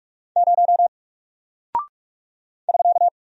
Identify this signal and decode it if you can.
Morse code, heard as a keyed tone, about 44 wpm, 700 Hz. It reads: 0 <SK>
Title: Single Character (Letter:Number:Punctuation:Prosign) 45wpm